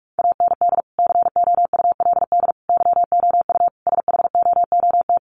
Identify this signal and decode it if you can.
Morse code, heard as a keyed tone, about 44 wpm, 700 Hz. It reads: AND COULD YOU SHOOT
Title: Sentences from Top 1200 Words 45wpm